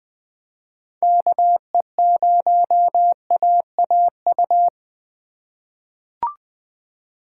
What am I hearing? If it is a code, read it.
Morse code, heard as a keyed tone, about 20 wpm, 700 Hz. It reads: KE0AAU
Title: USA Call Signs - 4x Character Spacing - 20wpm